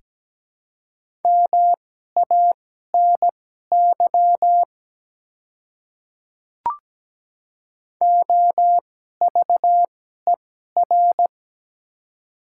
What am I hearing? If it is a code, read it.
Morse code, heard as a keyed tone, about 17 wpm, spaced as for 12 wpm, 700 Hz. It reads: MANY OVER
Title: 2x Character Spacing - Top 100 Words - 17wpm